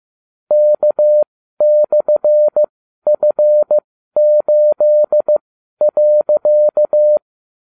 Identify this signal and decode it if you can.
Morse code, heard as a keyed tone, about 15 wpm, spaced as for 12 wpm, 600 Hz. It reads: K/F8.